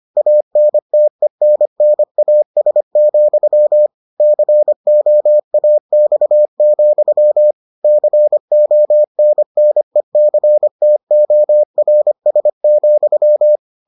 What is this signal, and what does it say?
Morse code, heard as a keyed tone, about 25 wpm, 600 Hz. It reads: ANTENNAS, COAX, CONNECTORS,